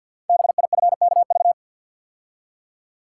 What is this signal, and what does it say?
Morse code, heard as a keyed tone, about 52 wpm, 700 Hz. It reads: BIFKV
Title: Code Groups - 5 Letter:Numbers - 50wpm